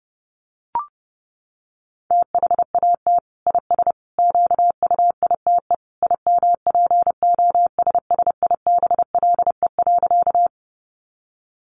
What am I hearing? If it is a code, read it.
Morse code, heard as a keyed tone, about 30 wpm, 700 Hz. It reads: THAT IS QUITE IMPOSSIBLE.
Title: Sentences from Top 1100 Words 30wpm